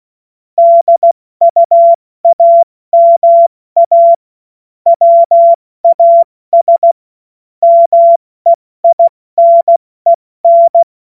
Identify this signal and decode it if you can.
Morse code, heard as a keyed tone, about 16 wpm, 680 Hz. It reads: DUAMA WAS MEINEN
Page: cw-witze